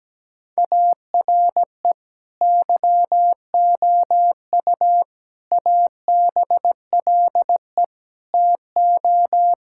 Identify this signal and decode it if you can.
Morse code, heard as a keyed tone, about 17 wpm, 700 Hz. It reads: ARE YOU ABLE TO
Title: Sentences from Top 1000 Words - Review Part 6 - 17wpm